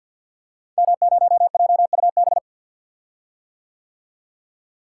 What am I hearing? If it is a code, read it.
Morse code, heard as a keyed tone, about 46 wpm, 690 Hz. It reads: M0JUB
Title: Code Groups - 5 Letter:Numbers - 50wpm